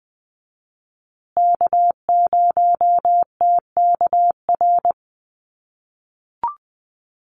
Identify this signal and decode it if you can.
Morse code, heard as a keyed tone, about 20 wpm, 700 Hz. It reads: K0TKR
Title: USA Call Signs Repeated 2x 20wpm